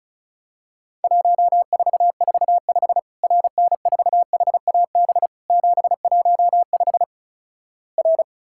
Morse code, heard as a keyed tone, about 36 wpm, 700 Hz. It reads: 1445 RN4HAB 715 R
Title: Worked All Europe DX Contest - QTC Exchange - 35wpm